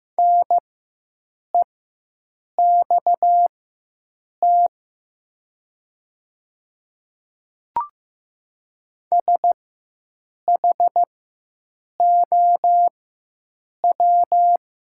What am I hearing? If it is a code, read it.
Morse code, heard as a keyed tone, about 15 wpm, spaced as for 7 wpm, 700 Hz. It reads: NEXT SHOW
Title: 4x Character Spacing - Top 300 Words - 15wpm